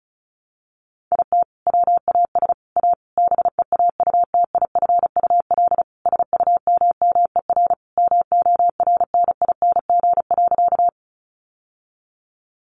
Morse code, heard as a keyed tone, about 34 wpm, 700 Hz. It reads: IT WAS A BEAUTIFUL SUMMER MORNING.
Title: Sentences from Top 1100 Words 35wpm